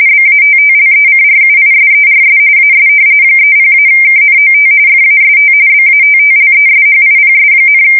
RTTY, about 45.45 baud. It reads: BATTLECRUISERS AS 30-KNOT FAST BATTLESHIPS,
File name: fsk170 / battleshipnj